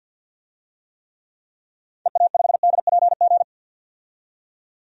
Morse code, heard as a keyed tone, about 50 wpm, 700 Hz. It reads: EA5DPG
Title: DXCC-100 Call Signs - 6 Characters - 50wpm